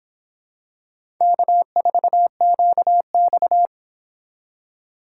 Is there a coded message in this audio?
Yes: Morse code from 1.2 to 3.7 s, heard as a keyed tone, about 26 wpm, 700 Hz. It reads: K4QX